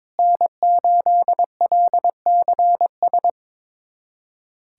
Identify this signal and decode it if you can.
Morse code, heard as a keyed tone, about 22 wpm, 700 Hz. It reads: N8LCS
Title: USA Callsign DE Callsign - 5 Characters - 22wpm